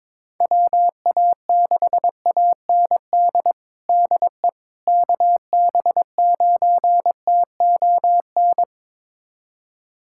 Morse code, heard as a keyed tone, about 22 wpm, 700 Hz. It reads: WA6AND DE KB9TON
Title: USA Callsign DE Callsign 22wpm